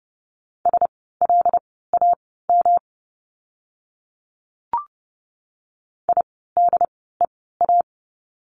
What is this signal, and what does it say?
Morse code, heard as a keyed tone, about 30 wpm, spaced as for 17 wpm, 700 Hz. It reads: SLAM IDEA
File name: course / 30:10wpm / 108 Lesson 18f - Letter m - Words 30wpm